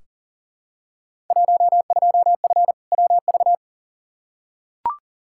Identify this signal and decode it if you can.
Morse code, heard as a keyed tone, about 40 wpm, 700 Hz. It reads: 12F WV
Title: Field Day Exchanges Encore - 40wpm